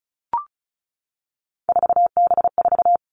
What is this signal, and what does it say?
Morse code, heard as a keyed tone, about 36 wpm, 700 Hz. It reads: 4B4
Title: Three Challenging Letter-Number Combinations 35wpm